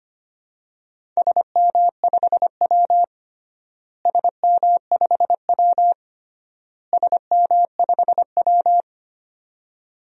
Morse code, heard as a keyed tone, about 25 wpm, 700 Hz. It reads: SM5W SM5W SM5W
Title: DXCC-100 Call Signs - 4 Characters - Repeated 3x - 25wpm